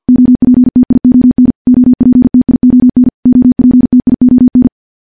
Morse code, heard as a keyed tone, about 51 wpm, 250 Hz. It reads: OPTION OPTION OPTION
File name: morse / morse-training-50WPM